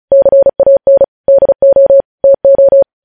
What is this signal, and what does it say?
Morse code, heard as a keyed tone, about 36 wpm, 550 Hz. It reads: CAN DO TO
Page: News headlines in Morse code at 35 wpm